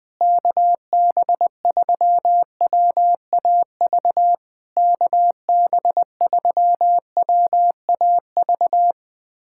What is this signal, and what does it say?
Morse code, heard as a keyed tone, about 20 wpm, 700 Hz. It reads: KB3WAV KB3WAV
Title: POTA QSO Practice - 20wpm